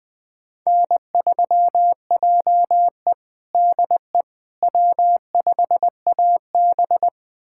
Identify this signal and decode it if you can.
Morse code, heard as a keyed tone, about 20 wpm, 700 Hz. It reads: N3JE DE W5AB
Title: USA Callsign DE Callsign - 4 Characters - 20wpm